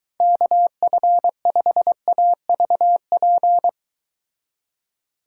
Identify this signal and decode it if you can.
Morse code, heard as a keyed tone, about 23 wpm, 700 Hz. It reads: KF5AVP